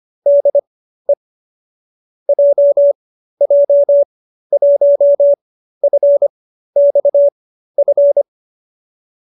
Morse code, heard as a keyed tone, about 25 wpm, spaced as for 13 wpm, 590 Hz. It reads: DE JJ1FXF
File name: qso208b 13